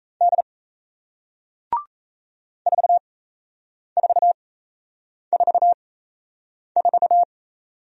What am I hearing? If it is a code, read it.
Morse code, heard as a keyed tone, about 37 wpm, spaced as for 8 wpm, 700 Hz. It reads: D 4444